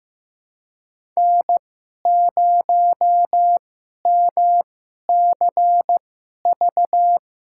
Morse code, heard as a keyed tone, about 15 wpm, spaced as for 11 wpm, 700 Hz. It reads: N0MCV
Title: USA Call Signs - 5 Characters - 2x Character Spacing - 15wpm